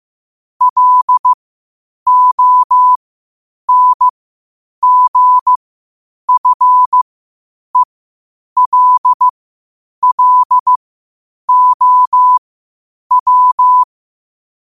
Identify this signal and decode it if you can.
Morse code, heard as a keyed tone, about 15 wpm, spaced as for 8 wpm, 1000 Hz. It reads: LONGFELLOW
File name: qotdfiles / 8wpm